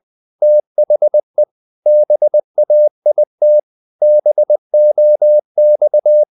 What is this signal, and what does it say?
Morse code, heard as a keyed tone, about 20 wpm, 600 Hz. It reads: THE BAIT BOX
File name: the old man and the sea / the old man and the sea part 1 20wpm